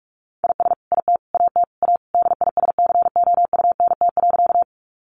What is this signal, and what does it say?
Morse code, heard as a keyed tone, about 44 wpm, 700 Hz. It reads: IS IT AT A DISCOUNT.
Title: Sentences from Top 1200 Words 45wpm